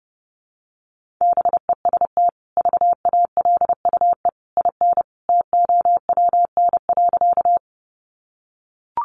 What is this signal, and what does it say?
Morse code, heard as a keyed tone, about 30 wpm, 700 Hz. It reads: BEST VALUE IN TOWN.
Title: Sentences from Top 600 Words 30wpm